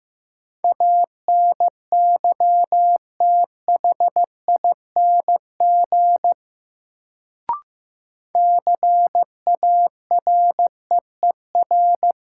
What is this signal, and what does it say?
Morse code, heard as a keyed tone, about 15 wpm, 700 Hz. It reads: ANYTHING CAREER
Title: Sentences from Top 1000 Words - New Word Drill - 15wpm